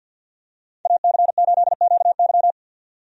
Morse code, heard as a keyed tone, about 51 wpm, 700 Hz. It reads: AC8QY
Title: USA Call Signs - 5 Characters - 50wpm